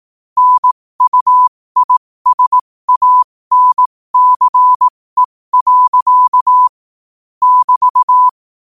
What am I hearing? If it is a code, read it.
Morse code, heard as a keyed tone, about 18 wpm, spaced as for 15 wpm, 1000 Hz. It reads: NUISANCE. <BT>